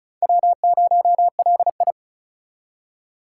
Morse code, heard as a keyed tone, about 36 wpm, 700 Hz. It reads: W0LI
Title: USA Call Signs - 3x Character Spacing - 35wpm